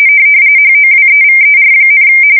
RTTY, about 45.45 baud. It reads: HOWEVER,
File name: fsk170 / battleshipnj